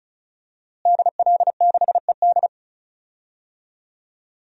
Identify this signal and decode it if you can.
Morse code, heard as a keyed tone, about 36 wpm, 700 Hz. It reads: DL6ED
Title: DXCC-100 Call Signs - 5 Characters - Repeated 2x - 35wpm